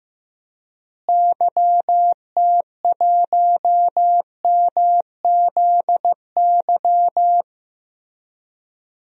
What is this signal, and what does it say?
Morse code, heard as a keyed tone, about 15 wpm, 700 Hz. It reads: YT1MZY
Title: DXCC-100 Call Signs - 6 Characters - Repeated 2x - 15wpm